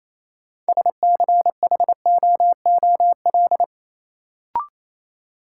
Morse code, heard as a keyed tone, about 28 wpm, 700 Hz. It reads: SCHOOL
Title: Sets of 2 Words - New Word Drill - 28wpm